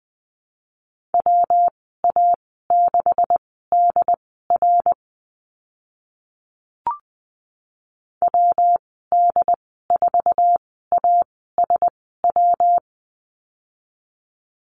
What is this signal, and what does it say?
Morse code, heard as a keyed tone, about 20 wpm, spaced as for 14 wpm, 700 Hz. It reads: WA6DR WD4ASW
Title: 140 Lesson 23g - Letter f - Call Signs 20wpm